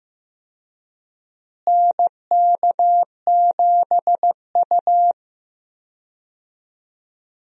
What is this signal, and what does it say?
Morse code, heard as a keyed tone, about 15 wpm, 700 Hz. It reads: NK7U
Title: USA Call Signs - 4 Characters - Repeated 3x - 15wpm